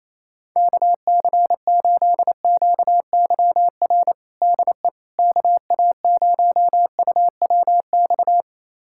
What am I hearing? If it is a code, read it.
Morse code, heard as a keyed tone, about 28 wpm, 700 Hz. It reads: KC8QYR DE KA0UWX
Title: USA Callsign DE Callsign - 6 Characters - 28wpm